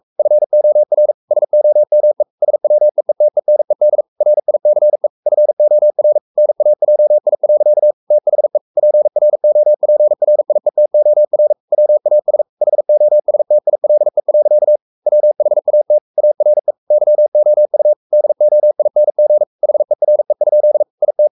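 Morse code, heard as a keyed tone, about 44 wpm, 600 Hz. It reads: FOR SOME SWEETENED RICE FOR NAJI. THE PROPRIETOR WAS HOSTILE. WHAT ARE YOU DOING HERE? IT
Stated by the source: Morse code at 43 wpm sending Ken Follett's novel Never in English, from CW med Gnister.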